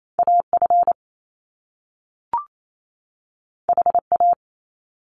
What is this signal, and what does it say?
Morse code, heard as a keyed tone, about 28 wpm, 700 Hz. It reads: AF HA
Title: Top Two Letter Combinations in Words 28wpm